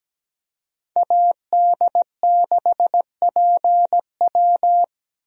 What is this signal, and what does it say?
Morse code, heard as a keyed tone, about 17 wpm, 700 Hz. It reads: AD6PW
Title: USA Call Signs - 5 Characters - Repeated 2x - 17wpm